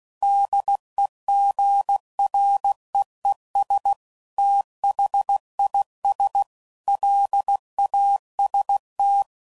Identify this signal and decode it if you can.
Morse code, heard as a keyed tone, about 16 wpm, 800 Hz. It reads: DEGREES THIS LAST
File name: CW Code Groups @15WPM / QSOs / WD0FBT